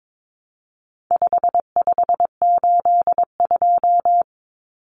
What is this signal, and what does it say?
Morse code, heard as a keyed tone, about 22 wpm, 700 Hz. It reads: 5582
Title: Numbers - 4-Digit 22wpm